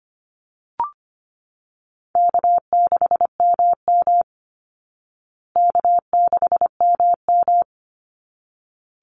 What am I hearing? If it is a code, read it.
Morse code, heard as a keyed tone, about 25 wpm, 700 Hz. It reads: K6MM K6MM
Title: Easy Call Signs Repeated 2x - 25wpm